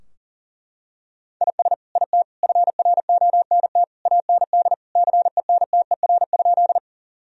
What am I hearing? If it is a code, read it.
Morse code, heard as a keyed tone, about 40 wpm, 700 Hz. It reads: IS IT FRONT AND CENTER?
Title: Sentences from Top 1000 Words - Review Part 2 - 40wpm